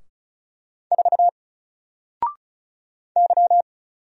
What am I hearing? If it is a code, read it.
Morse code, heard as a keyed tone, about 36 wpm, 700 Hz. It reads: 4 Y
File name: Single-Letter-Number-35wpm